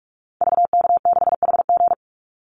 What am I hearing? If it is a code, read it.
Morse code, heard as a keyed tone, about 44 wpm, 700 Hz. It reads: VK6HG
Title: Call Signs Repeated 2x - 45wpm